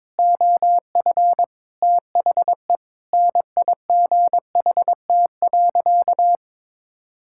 Morse code, heard as a keyed tone, about 22 wpm, 700 Hz. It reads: OF THE NIGHT.
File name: Sentences with 7-Word Phrases - 22wpm